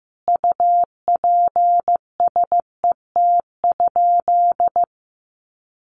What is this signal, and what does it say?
Morse code, heard as a keyed tone, about 15 wpm, 700 Hz. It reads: UPSET?
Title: Sentences from Top 800 Words 15wpm